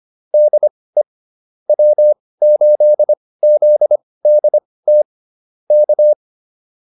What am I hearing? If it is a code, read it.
Morse code, heard as a keyed tone, about 25 wpm, spaced as for 18 wpm, 600 Hz. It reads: DE W8ZDT K